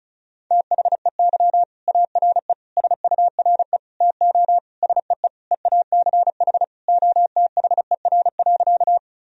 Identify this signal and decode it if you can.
Morse code, heard as a keyed tone, about 34 wpm, 700 Hz. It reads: THEY ARE SURE TO SEE EACH OTHER.
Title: Sentences from Top 1000 Words - Review Part 1 - 35wpm